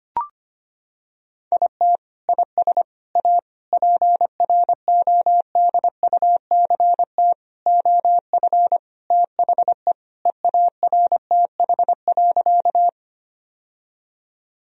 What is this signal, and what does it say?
Morse code, heard as a keyed tone, about 25 wpm, 700 Hz. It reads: IT IS A PRODUCT OF THE EARTH.
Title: Sentences from Top 1000 Words - Review Part 2 - 25wpm